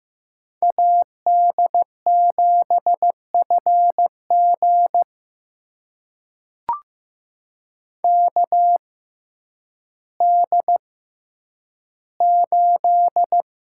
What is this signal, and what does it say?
Morse code, heard as a keyed tone, about 15 wpm, spaced as for 9 wpm, 700 Hz. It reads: AD7FG KD8
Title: USA Call Signs - 5 Characters - 6x Character Spacing - 15wpm